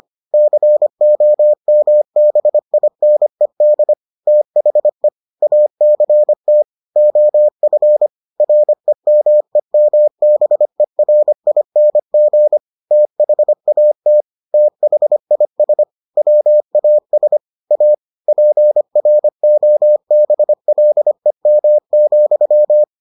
Morse code, heard as a keyed tone, about 25 wpm, 600 Hz. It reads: COMBINED THE ACT OF REMEMBERING THAT THIS WAS A PROBLEM,